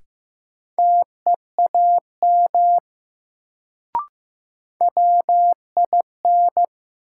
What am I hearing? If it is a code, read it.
Morse code, heard as a keyed tone, about 16 wpm, 700 Hz. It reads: TEAM WIN
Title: Sentences from Top 1500 Words - New Word Drill - 15wpm